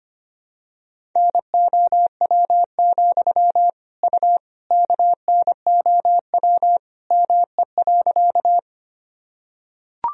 Morse code, heard as a keyed tone, about 25 wpm, 700 Hz. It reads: NOW, U KNOW ME.
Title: Sentences from Top 200 Words and Abbreviations 25wpm